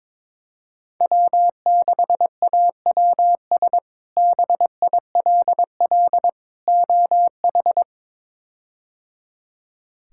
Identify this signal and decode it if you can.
Morse code, heard as a keyed tone, about 22 wpm, 700 Hz. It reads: W6AWS BILL OH
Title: K1USN SST QSO Practice - 22wpm